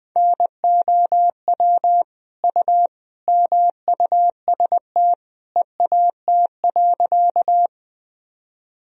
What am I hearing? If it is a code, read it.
Morse code, heard as a keyed tone, about 20 wpm, 700 Hz. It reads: NOW U MUST EAT.
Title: Sentences from Top 300 Words and Abbreviations 20wpm